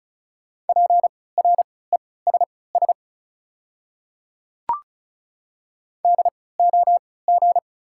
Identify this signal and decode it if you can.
Morse code, heard as a keyed tone, about 36 wpm, spaced as for 20 wpm, 690 Hz. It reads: PRESS DOG
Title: 3x Character Spacing - Top 500 Words - 35wpm